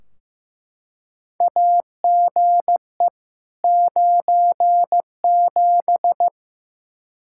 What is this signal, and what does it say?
Morse code, heard as a keyed tone, about 15 wpm, 700 Hz. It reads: AGE 97
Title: Practice QSO - Exchange 3 - More Info - 15wpm